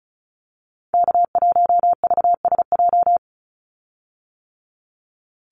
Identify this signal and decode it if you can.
Morse code, heard as a keyed tone, about 34 wpm, 700 Hz. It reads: K1VSJ
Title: Call Signs - 35wpm